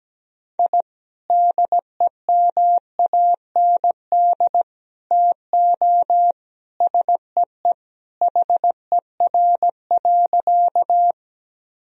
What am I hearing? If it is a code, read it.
Morse code, heard as a keyed tone, about 17 wpm, 700 Hz. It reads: I DEMAND TO SEE HER.
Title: Sentences from Top 1000 Words - Review Part 4 - 17wpm